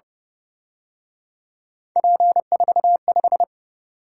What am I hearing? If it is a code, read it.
Morse code, heard as a keyed tone, about 29 wpm, 700 Hz. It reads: P45
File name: Speed-Racing - Three Letter-Number Combinations - 25wpm